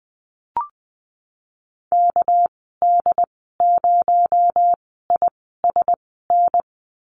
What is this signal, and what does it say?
Morse code, heard as a keyed tone, about 20 wpm, spaced as for 15 wpm, 700 Hz. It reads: KD0ISN
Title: 231 Lesson 37g - Letter x - Call Signs 20wpm